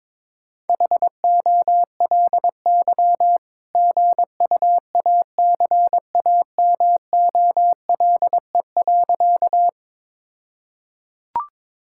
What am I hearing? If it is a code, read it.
Morse code, heard as a keyed tone, about 22 wpm, 700 Hz. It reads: HOLY GUACAMOLE.
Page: morse-code-ninja-practice-22wpm